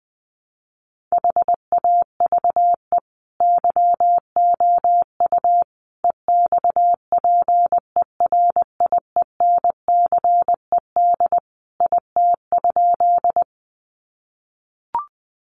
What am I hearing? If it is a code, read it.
Morse code, heard as a keyed tone, about 20 wpm, 700 Hz. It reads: HAVE YOU EXPERIENCED IT?
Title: Sentences from Top 1100 Words 20wpm